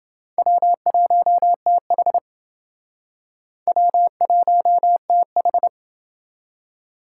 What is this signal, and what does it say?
Morse code, heard as a keyed tone, about 29 wpm, 700 Hz. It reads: W1TH W1TH